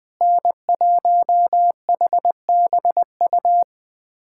Morse code, heard as a keyed tone, about 20 wpm, 700 Hz. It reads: N1HBU
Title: USA Callsign DE Callsign - 5 Characters - 20wpm